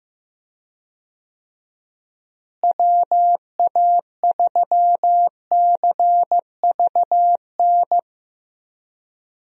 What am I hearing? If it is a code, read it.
Morse code, heard as a keyed tone, about 15 wpm, 700 Hz. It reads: WA3CVN
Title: USA Call Signs - 6 Characters - Repeated 3x - 15wpm